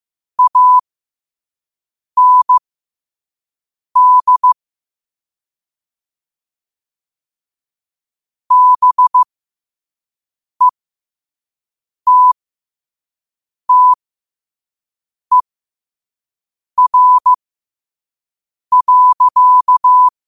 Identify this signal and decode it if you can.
Morse code, heard as a keyed tone, about 15 wpm, spaced as for 5 wpm, 1000 Hz. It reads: AND BETTER.
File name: qotdfiles / 5wpm